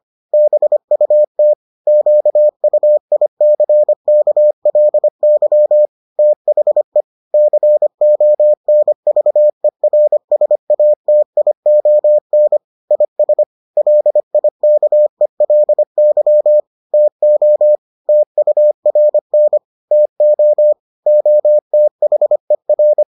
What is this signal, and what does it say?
Morse code, heard as a keyed tone, about 25 wpm, 600 Hz. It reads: BUT QUICKLY THE CONVERSATION IS LIKELY TO TURN TO OTHER